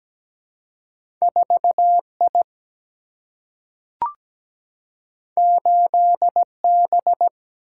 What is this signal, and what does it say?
Morse code, heard as a keyed tone, about 17 wpm, 700 Hz. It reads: 4I 8B
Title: Single-Letter Single-Number 17wpm